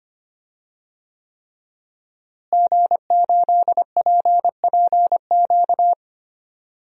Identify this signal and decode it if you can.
Morse code, heard as a keyed tone, about 25 wpm, 700 Hz. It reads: G8PPQ
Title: DXCC-100 Call Signs - 5 Characters - Repeated 3x - 25wpm